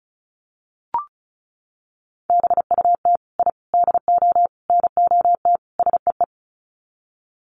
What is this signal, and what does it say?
Morse code, heard as a keyed tone, about 34 wpm, 700 Hz. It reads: BUT I DO NOT SEE
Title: Sets of 5 Words Encore 35wpm